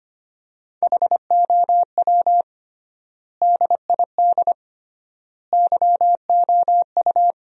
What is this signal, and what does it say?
Morse code, heard as a keyed tone, about 25 wpm, 700 Hz. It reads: HOW DID YOU
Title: 3x Word-Spacing - Sentences from Top 100 Words - 25wpm